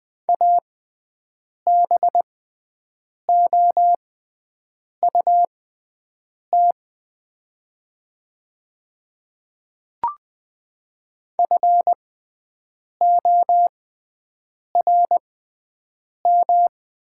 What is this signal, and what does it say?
Morse code, heard as a keyed tone, about 20 wpm, spaced as for 7 wpm, 700 Hz. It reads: ABOUT FORM
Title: 6x Character Spacing - Top 300 Words - 20wpm